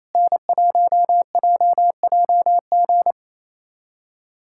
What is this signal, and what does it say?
Morse code, heard as a keyed tone, about 28 wpm, 700 Hz. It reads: N1JJG
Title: USA Callsign DE Callsign 28wpm